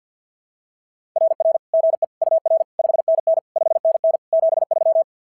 Morse code, heard as a keyed tone, about 46 wpm, 660 Hz. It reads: RR GE UR 5NN 5NN 73
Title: SOTA QSO Practice 50wpm